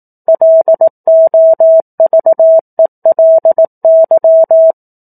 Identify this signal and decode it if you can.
Morse code, heard as a keyed tone, about 18 wpm, 650 Hz. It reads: LOVELY
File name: 18 WPM